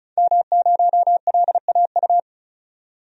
Morse code, heard as a keyed tone, about 36 wpm, 700 Hz. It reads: M0LAU
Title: DXCC-100 Call Signs - 5 Characters - 35wpm